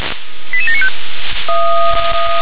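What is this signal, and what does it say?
Five-tone selective call (ZVEI), heard as quick runs of five tones, tones of about 70 ms. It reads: 88795
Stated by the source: ZVEI five-tone selective call received by SDR.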